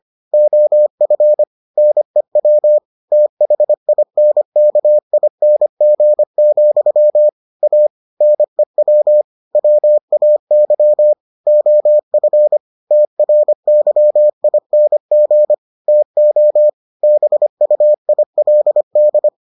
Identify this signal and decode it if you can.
Morse code, heard as a keyed tone, about 25 wpm, 600 Hz. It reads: OF NEW THINKING, A NEW WAY OF TRYING TO BUILD